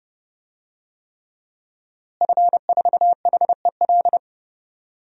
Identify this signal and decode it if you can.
Morse code, heard as a keyed tone, about 30 wpm, 700 Hz. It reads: F4HEL
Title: DXCC-100 Call Signs - 5 Characters - Repeated 3x - 30wpm